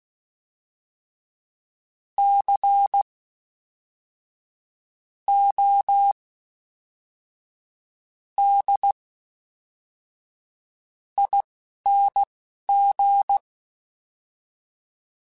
Morse code, heard as a keyed tone, about 16 wpm, spaced as for 5 wpm, 800 Hz. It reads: CODING
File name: Special INGs / Special INGs-mp3s / Fast ing @10